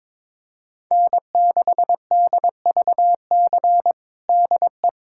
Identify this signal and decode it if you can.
Morse code, heard as a keyed tone, about 21 wpm, 700 Hz. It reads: N6DVC DE
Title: USA Callsign DE Callsign 22wpm